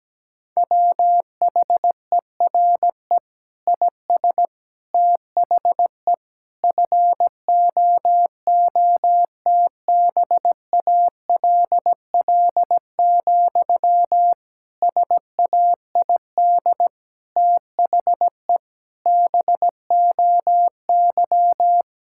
Morse code, heard as a keyed tone, about 17 wpm, 700 Hz. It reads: WHERE IS THE FOOTBALL, SAID THE BOY
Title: Sentences from Top 1600 Words 17wpm